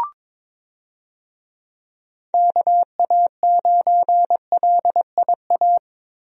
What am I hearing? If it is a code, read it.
Morse code, heard as a keyed tone, about 22 wpm, 700 Hz. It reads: KA9LIA
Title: USA Call Signs - 6 Characters - Repeated 2x - 22wpm